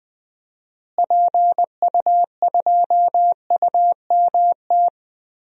Morse code, heard as a keyed tone, about 20 wpm, 700 Hz. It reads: PU2UMT
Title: DXCC-100 Call Signs - 6 Characters - Repeated 3x - 20wpm